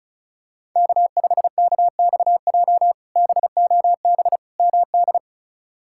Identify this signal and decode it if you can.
Morse code, heard as a keyed tone, about 36 wpm, 700 Hz. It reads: K5KXJ BOB MD
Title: K1USN SST QSO Practice - 35wpm